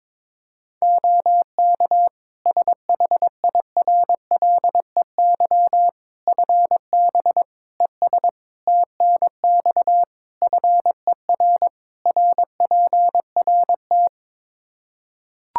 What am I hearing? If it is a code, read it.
Morse code, heard as a keyed tone, about 22 wpm, 700 Hz. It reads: OK SHIRLEY FB ES TNX FER RPRT
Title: Practice QSO - Exchange 2 - Info - 22wpm